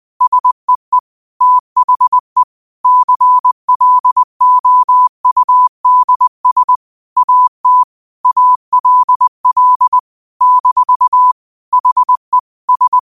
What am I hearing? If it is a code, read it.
Morse code, heard as a keyed tone, about 20 wpm, 1000 Hz. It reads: SEE THE CLOUDS AT ALL - HE S
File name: qotdfiles / 20wpm